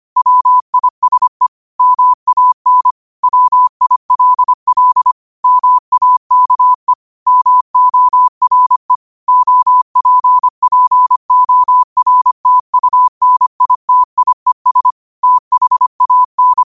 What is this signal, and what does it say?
Morse code, heard as a keyed tone, about 25 wpm, 1000 Hz. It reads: WISE MAN WILL MAKE MORE OPPORTUNITIES THAN
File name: qotdfiles / 25wpm